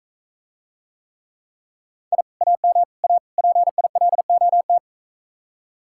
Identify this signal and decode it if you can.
Morse code, heard as a keyed tone, about 45 wpm, 700 Hz. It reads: I AM A PILOT